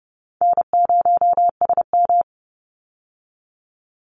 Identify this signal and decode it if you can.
Morse code, heard as a keyed tone, about 30 wpm, 700 Hz. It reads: N0SM
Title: Easy Call Signs - 30wpm